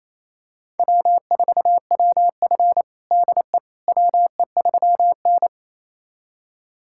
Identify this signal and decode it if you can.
Morse code, heard as a keyed tone, about 28 wpm, 700 Hz. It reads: W4WF DE WE3N
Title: USA Callsign DE Callsign - 4 Characters - 28wpm